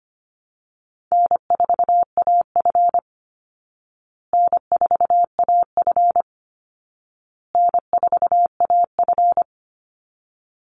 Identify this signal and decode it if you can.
Morse code, heard as a keyed tone, about 25 wpm, 700 Hz. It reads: N4AF N4AF N4AF
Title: Easy Call Signs Repeated 3x - 25wpm